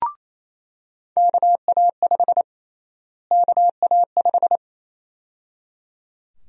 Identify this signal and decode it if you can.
Morse code, heard as a keyed tone, about 27 wpm, 700 Hz. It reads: KA5 KA5
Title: USA Call Sign Prefixes - Repeated 2x - 28wpm